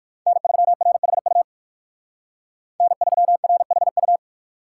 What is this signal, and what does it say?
Morse code, heard as a keyed tone, about 59 wpm, 700 Hz. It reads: N3RHU N3RHU